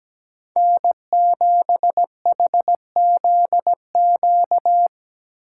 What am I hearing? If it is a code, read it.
Morse code, heard as a keyed tone, about 17 wpm, 700 Hz. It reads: N7HZQ
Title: USA Call Signs - 5 Characters - 4x Character Spacing - 17wpm